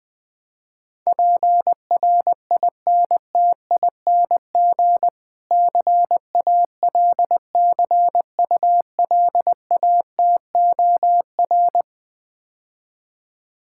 Morse code, heard as a keyed tone, about 20 wpm, 700 Hz. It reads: PRINTING CALCULATOR
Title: Difficult Words - Speed-Racing with Hints - 20wpm